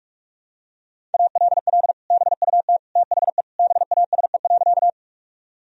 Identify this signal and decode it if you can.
Morse code, heard as a keyed tone, about 46 wpm, 700 Hz. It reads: ALL BUT THE BASE.
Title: Sentences from Top 1000 Words - Review Part 1 - 45wpm